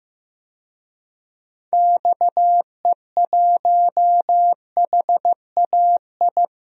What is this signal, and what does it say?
Morse code, heard as a keyed tone, about 15 wpm, 700 Hz. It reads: XE1HAI
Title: DXCC-100 Call Signs - 6 Characters - Repeated 3x - 15wpm